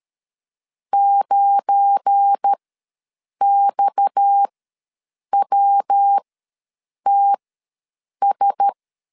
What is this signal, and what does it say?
Morse code, heard as a keyed tone, about 13 wpm, spaced as for 7 wpm, 800 Hz. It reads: 9XWTS
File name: LEZIONI / 7 GF9Z / 7 GF9 ws 12wpm 3344